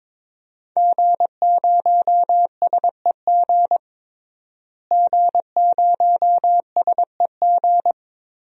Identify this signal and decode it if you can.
Morse code, heard as a keyed tone, about 21 wpm, 700 Hz. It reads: G0SEG G0SEG
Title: DXCC-100 Call Signs - 5 Characters - Repeated 3x - 22wpm